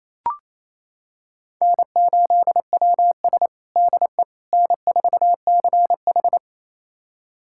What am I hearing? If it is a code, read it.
Morse code, heard as a keyed tone, about 28 wpm, 700 Hz. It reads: N8WS DE N4CH
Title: USA Callsign DE Callsign - 4 Characters - 28wpm